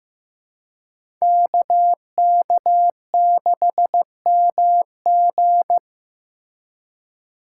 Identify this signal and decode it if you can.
Morse code, heard as a keyed tone, about 15 wpm, 700 Hz. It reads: KK6MG
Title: USA Call Signs - 5 Characters - 3x Character Spacing - 15wpm